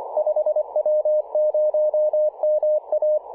Morse code, heard as a keyed tone, about 24 wpm, 590 Hz. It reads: 5W0MA
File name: 5W0MA 18.077MHz CW